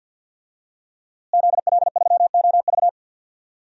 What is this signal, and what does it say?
Morse code, heard as a keyed tone, about 47 wpm, 700 Hz. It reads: ZL3OV
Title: DXCC-100 Call Signs - 5 Characters - Repeated 3x - 50wpm